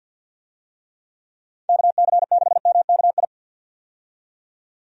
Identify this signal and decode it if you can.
Morse code, heard as a keyed tone, about 47 wpm, 700 Hz. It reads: KC6MKI
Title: USA Call Signs - 6 Characters - 50wpm